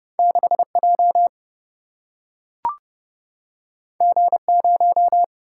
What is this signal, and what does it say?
Morse code, heard as a keyed tone, about 30 wpm, 700 Hz. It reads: 6 J G 0
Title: Single-Letter Single-Number 30wpm